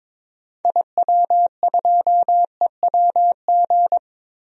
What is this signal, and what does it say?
Morse code, heard as a keyed tone, about 22 wpm, 700 Hz. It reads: IW2EWG
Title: DXCC-100 Call Signs - 6 Characters - Repeated 3x - 22wpm